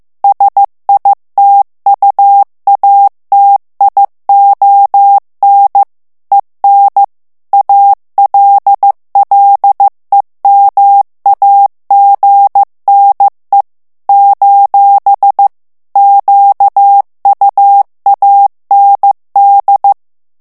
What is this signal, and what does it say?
Morse code, heard as a keyed tone, about 15 wpm, 800 Hz. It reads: SITUATION EN ALLEMAGNE : QUAND
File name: morse mp3 / FRANCAIS TEXTE CLAIR 1 VITESSE 14 MOTS MINUTES